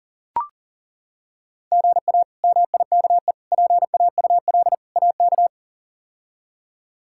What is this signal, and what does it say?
Morse code, heard as a keyed tone, about 40 wpm, 700 Hz. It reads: GA MIKE PAUL AK
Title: K1USN SST QSO Practice - 40wpm